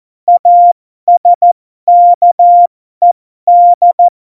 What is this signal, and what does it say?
Morse code, heard as a keyed tone, about 14 wpm, spaced as for 12 wpm, 700 Hz. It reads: ASKED
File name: Morsekurs / Witze